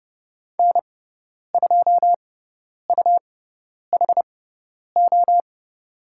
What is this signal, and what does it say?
Morse code, heard as a keyed tone, about 30 wpm, spaced as for 10 wpm, 700 Hz. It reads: N2UHO
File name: Lesson 17g - Letter u - Call Signs 30wpm